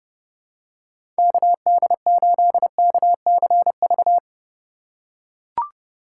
Morse code, heard as a keyed tone, about 30 wpm, 700 Hz. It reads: KD8KCV
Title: USA Call Signs - 4x Character Spacing - 30wpm